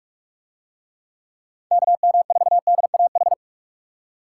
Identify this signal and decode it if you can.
Morse code, heard as a keyed tone, about 49 wpm, 700 Hz. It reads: KM4DAH